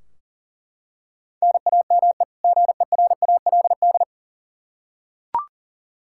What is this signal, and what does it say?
Morse code, heard as a keyed tone, about 40 wpm, 700 Hz. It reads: NAME GERALD
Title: Practice QSO - Exchange 1 - Beginning - 40wpm